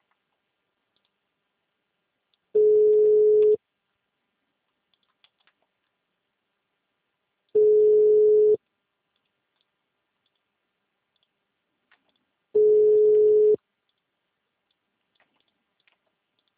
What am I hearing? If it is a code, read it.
Ringback tone (Europe, 425 Hz).